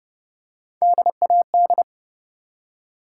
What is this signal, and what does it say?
Morse code, heard as a keyed tone, about 30 wpm, 700 Hz. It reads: DAD
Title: Three Letter Words - 30wpm